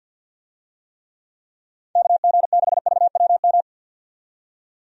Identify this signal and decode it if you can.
Morse code, heard as a keyed tone, about 52 wpm, 700 Hz. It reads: KG6VWM
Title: USA Call Signs - 6 Characters - 50wpm